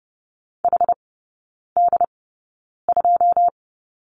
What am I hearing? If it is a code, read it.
Morse code, heard as a keyed tone, about 30 wpm, spaced as for 9 wpm, 700 Hz. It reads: HD2
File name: Lesson 19e - Letter w - Three Characters 30wpm